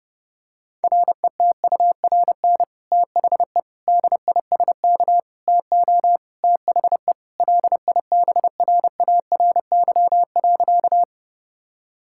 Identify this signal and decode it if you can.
Morse code, heard as a keyed tone, about 30 wpm, 700 Hz. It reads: RETURN THE DISK TO THE LIBRARY.
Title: Sentences from Top 1000 Words - Review Part 6 - 30wpm